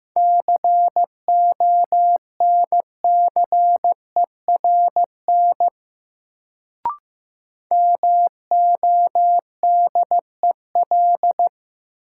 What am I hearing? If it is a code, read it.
Morse code, heard as a keyed tone, about 15 wpm, 700 Hz. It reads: CONCERN MODEL
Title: Sentences from Top 1000 Words - New Word Drill - 15wpm